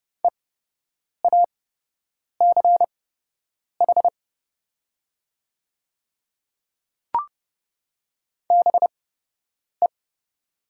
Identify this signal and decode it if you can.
Morse code, heard as a keyed tone, about 29 wpm, spaced as for 8 wpm, 700 Hz. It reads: EACH BE
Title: 8x Character Spacing - Top 100 Words - 30wpm